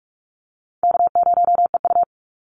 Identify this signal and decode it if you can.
Morse code, heard as a keyed tone, about 46 wpm, 700 Hz. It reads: K0EU
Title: Easy Call Signs - 45wpm